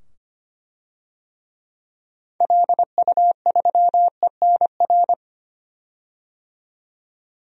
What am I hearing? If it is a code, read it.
Morse code, heard as a keyed tone, about 25 wpm, 700 Hz. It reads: LU3ENR
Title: DXCC-100 Call Signs - 6 Characters - 25wpm